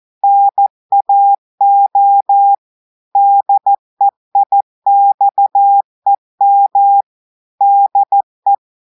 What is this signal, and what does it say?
Morse code, heard as a keyed tone, about 14 wpm, 800 Hz. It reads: NAO DEIXEM DE
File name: audios / portugais / portugais texte en clair 2 vitesse 14 mots minutes